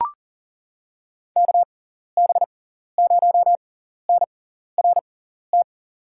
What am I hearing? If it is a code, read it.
Morse code, heard as a keyed tone, about 40 wpm, spaced as for 14 wpm, 700 Hz. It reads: KB0NRT
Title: USA Call Signs - 6 Characters - 6x Character Spacing - 40wpm